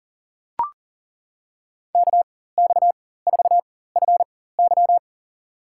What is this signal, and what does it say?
Morse code, heard as a keyed tone, about 38 wpm, spaced as for 18 wpm, 700 Hz. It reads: KX4FY
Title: USA Call Signs - 5 Characters - 4x Character Spacing - 40wpm